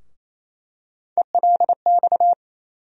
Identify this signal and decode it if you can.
Morse code, heard as a keyed tone, about 28 wpm, 700 Hz. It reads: ELX
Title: USA Call Sign Postfixes - 3 Letters 28wpm